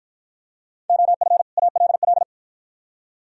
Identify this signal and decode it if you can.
Morse code, heard as a keyed tone, about 53 wpm, 690 Hz. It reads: OF ALL